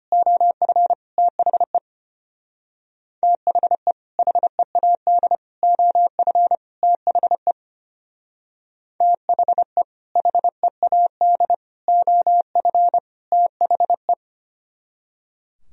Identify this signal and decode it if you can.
Morse code, heard as a keyed tone, about 30 wpm, 700 Hz. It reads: OF THE THE HEAD OF THE THE HEAD OF THE
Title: Speed-Racing - Sets of 4 Words 25wpm